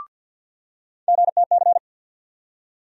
Morse code, heard as a keyed tone, about 53 wpm, 700 Hz. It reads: GTC
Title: USA Call Sign Postfixes - 3 Letters 50wpm